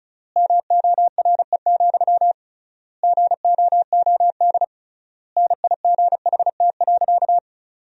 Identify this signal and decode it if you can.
Morse code, heard as a keyed tone, about 36 wpm, 700 Hz. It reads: MORE, GOOD NIGHT.
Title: 3x Word-Spacing - Sentences from Top 300 Words - 35wpm